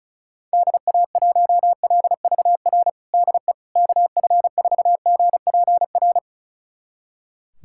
Morse code, heard as a keyed tone, about 36 wpm, 700 Hz. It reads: DA1LVR DE KF4GPR
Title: USA Callsign DE Callsign - 6 Characters - 35wpm